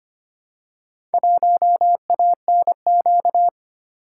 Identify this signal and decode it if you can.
Morse code, heard as a keyed tone, about 25 wpm, 700 Hz. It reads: 1ANQ
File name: practice / 25wpm / Four Letter-Number Combinations - 25wpm